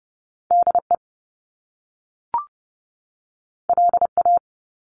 Morse code, heard as a keyed tone, about 30 wpm, 700 Hz. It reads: DE LA
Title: Top Two Letter Combinations in Words 30wpm